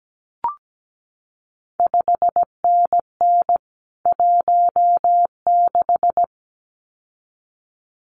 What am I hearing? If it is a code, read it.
Morse code, heard as a keyed tone, about 17 wpm, 700 Hz. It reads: 5NN 16
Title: CQ WW DX Contest Exchange - 17wpm